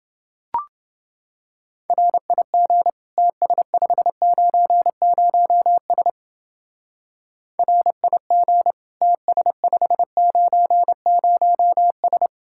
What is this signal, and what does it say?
Morse code, heard as a keyed tone, about 28 wpm, 700 Hz. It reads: RIG TS590S RIG TS590S